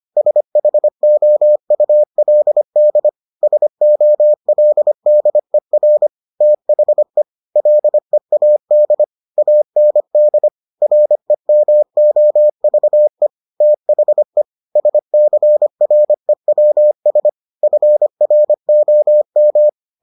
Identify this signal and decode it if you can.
Morse code, heard as a keyed tone, about 25 wpm, 600 Hz. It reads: SHOULD SOLDER THE LEAD AND REMOVE THE SCREWS FROM